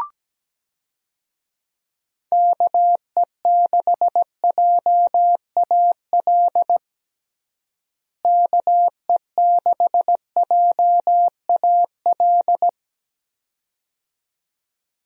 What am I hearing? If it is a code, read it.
Morse code, heard as a keyed tone, about 17 wpm, 700 Hz. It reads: KE6JAL KE6JAL
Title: USA Call Signs - 6 Characters - Repeated 2x - 17wpm